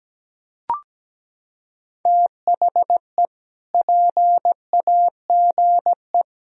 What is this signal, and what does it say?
Morse code, heard as a keyed tone, about 17 wpm, 700 Hz. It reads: THE PAGE
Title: Sentences from Top 1000 Words - Review Part 1 - 17wpm